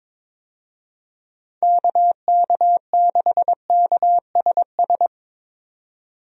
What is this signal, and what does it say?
Morse code, heard as a keyed tone, about 22 wpm, 700 Hz. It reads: KK6KSS
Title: USA Call Signs - 6 Characters - 22wpm